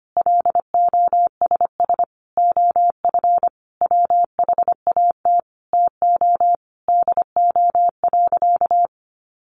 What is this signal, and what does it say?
Morse code, heard as a keyed tone, about 25 wpm, 700 Hz. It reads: LOSS OF WHAT TO DO.
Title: Sentences from Top 1400 Words 25wpm